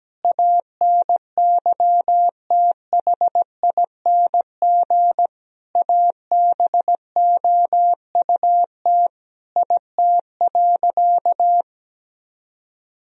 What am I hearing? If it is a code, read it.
Morse code, heard as a keyed tone, about 17 wpm, 700 Hz. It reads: ANYTHING ABOUT IT.
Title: Sentences from Top 1000 Words - Review Part 2 - 17wpm